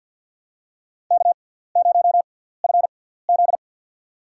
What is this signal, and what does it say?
Morse code, heard as a keyed tone, about 47 wpm, spaced as for 17 wpm, 700 Hz. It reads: K0FZ
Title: USA Call Signs - 4 Characters - 6x Character Spacing - 50wpm